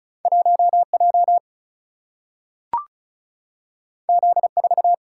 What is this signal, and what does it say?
Morse code, heard as a keyed tone, about 36 wpm, 700 Hz. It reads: 1J Z4
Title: Single-Letter Single-Number 35wpm